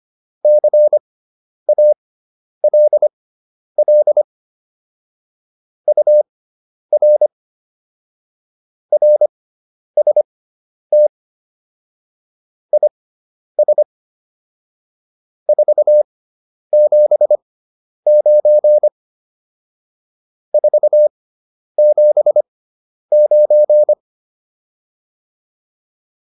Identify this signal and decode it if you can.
Morse code, heard as a keyed tone, about 25 wpm, spaced as for 10 wpm, 600 Hz. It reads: CALL UR RST IS 479 479
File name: QSO401 10